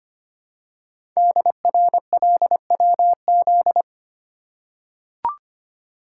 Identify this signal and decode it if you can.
Morse code, heard as a keyed tone, about 25 wpm, 700 Hz. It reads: DRLWZ
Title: Code Groups - 5 Letters - 25wpm